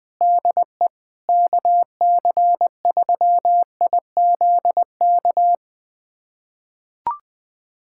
Morse code, heard as a keyed tone, about 20 wpm, 700 Hz. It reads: DE KC3IZK
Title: USA Callsign DE Callsign 20wpm